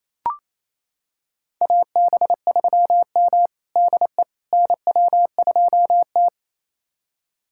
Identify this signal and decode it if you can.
Morse code, heard as a keyed tone, about 27 wpm, 690 Hz. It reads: AB3M DE NW2T
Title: USA Callsign DE Callsign - 4 Characters - 28wpm